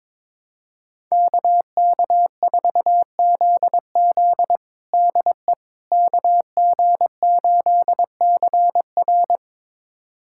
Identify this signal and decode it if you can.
Morse code, heard as a keyed tone, about 22 wpm, 700 Hz. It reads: KK4ZZ DE KG8CR
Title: USA Callsign DE Callsign - 5 Characters - 22wpm